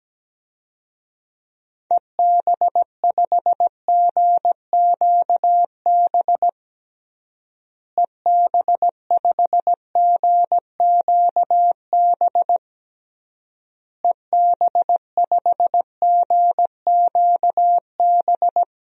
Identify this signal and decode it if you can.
Morse code, heard as a keyed tone, about 17 wpm, 700 Hz. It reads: EB5GQB EB5GQB EB5GQB
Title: DXCC-100 Call Signs - 6 Characters - Repeated 3x - 17wpm